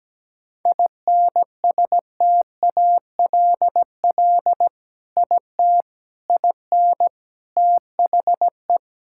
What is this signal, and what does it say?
Morse code, heard as a keyed tone, about 17 wpm, 700 Hz. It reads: INSTALL IT IN THE
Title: Sentences from Top 1600 Words 17wpm